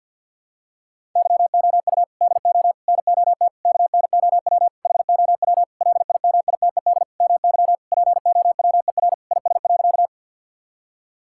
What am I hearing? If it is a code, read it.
Morse code, heard as a keyed tone, about 53 wpm, 700 Hz. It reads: YOU DO NOT KNOW HOW LIMITED MY POWER IS.
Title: Sentences from Top 1000 Words - Review Part 6 - 50wpm